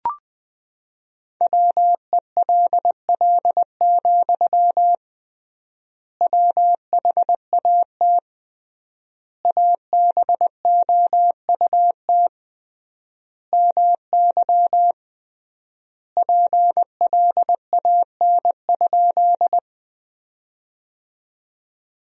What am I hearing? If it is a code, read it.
Morse code, heard as a keyed tone, about 21 wpm, 700 Hz. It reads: WELL, WHAT ABOUT MY PLAN?
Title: 3x Word-Spacing - Sentences from Top 500 Words - 20wpm